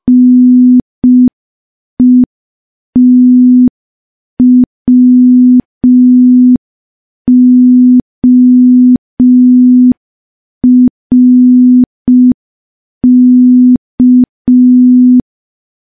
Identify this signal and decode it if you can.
Morse code, heard as a keyed tone, about 5 wpm, 250 Hz. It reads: NETWORK
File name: morse / morse-training-05WPM